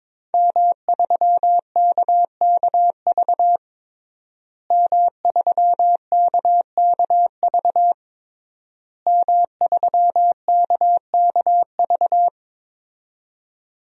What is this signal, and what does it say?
Morse code, heard as a keyed tone, about 22 wpm, 700 Hz. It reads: M3KKV M3KKV M3KKV
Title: DXCC-100 Call Signs - 5 Characters - Repeated 3x - 22wpm